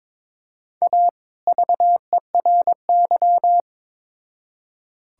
Morse code, heard as a keyed tone, about 22 wpm, 700 Hz. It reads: A VERY
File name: Build A Phrase - 2-Word Phrases - 22wpm